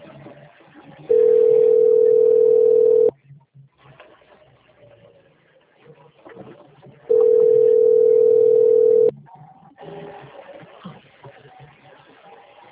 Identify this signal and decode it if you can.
Ringback tone (North America), 440 + 480 Hz.